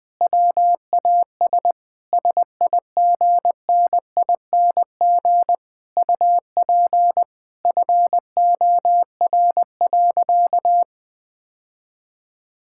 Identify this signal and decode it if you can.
Morse code, heard as a keyed tone, about 21 wpm, 700 Hz. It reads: WAS SIGNING UP FOR.
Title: Sentences with 7-Word Phrases - 20wpm